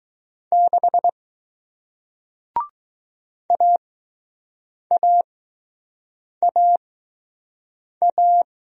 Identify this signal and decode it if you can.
Morse code, heard as a keyed tone, about 21 wpm, spaced as for 6 wpm, 700 Hz. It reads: 6 AAAA